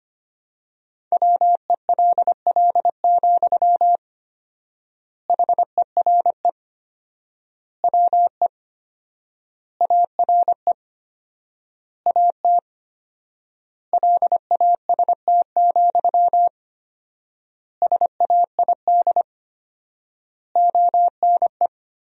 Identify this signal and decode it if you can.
Morse code, heard as a keyed tone, about 25 wpm, 700 Hz. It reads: WELL, HERE WE ARE AT LAST, SAID ONE
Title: 4x Word-Spacing - Sentences from Top 300 Words - 25wpm